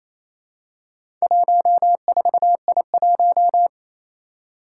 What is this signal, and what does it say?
Morse code, heard as a keyed tone, about 28 wpm, 700 Hz. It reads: 14I1
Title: Four Challenging Letter-Number Combinations 28wpm